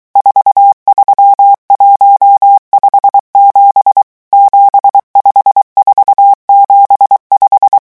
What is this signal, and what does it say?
Morse code, heard as a keyed tone, about 23 wpm, 800 Hz. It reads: 43157 75475
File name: morse mp3 / revision11 chiffres vitesse 22